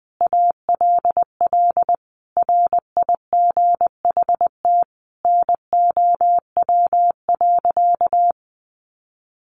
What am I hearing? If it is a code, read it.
Morse code, heard as a keyed tone, about 20 wpm, 700 Hz. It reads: ALL RIGHT NOW.
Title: Sentences from Top 700 Words 20wpm